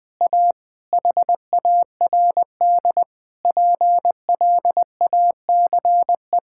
Morse code, heard as a keyed tone, about 20 wpm, 700 Hz. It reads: A HARD PLACE
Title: Sets of 8 Words - 20wpm